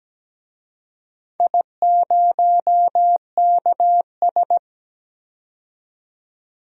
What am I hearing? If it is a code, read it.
Morse code, heard as a keyed tone, about 17 wpm, 700 Hz. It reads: I0KS
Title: DXCC-100 Call Signs - 4 Characters - 17wpm